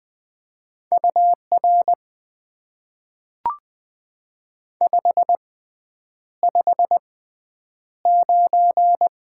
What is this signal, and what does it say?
Morse code, heard as a keyed tone, about 20 wpm, spaced as for 9 wpm, 700 Hz. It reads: UR 559
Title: Top 90 QSO Elements - 6x Character Spacing - 20wpm